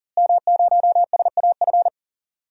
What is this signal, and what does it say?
Morse code, heard as a keyed tone, about 40 wpm, 700 Hz. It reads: M0SAF
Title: DXCC-100 Call Signs - 5 Characters - Repeated 2x - 40wpm